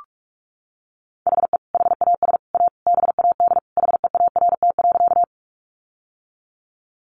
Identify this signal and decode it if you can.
Morse code, heard as a keyed tone, about 44 wpm, 700 Hz. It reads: HE HAS A BAD HEART.
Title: Sentences from Top 1200 Words 45wpm